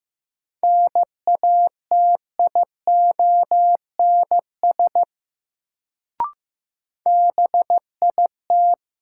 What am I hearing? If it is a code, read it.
Morse code, heard as a keyed tone, about 15 wpm, 700 Hz. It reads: NATIONS BIT
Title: Sets of 3 Words - New Word Drill - 15wpm